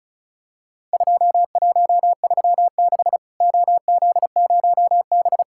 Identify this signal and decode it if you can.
Morse code, heard as a keyed tone, about 36 wpm, 700 Hz. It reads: 2136 OZ0B
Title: Worked All Europe DX Contest - QTC Exchange - 35wpm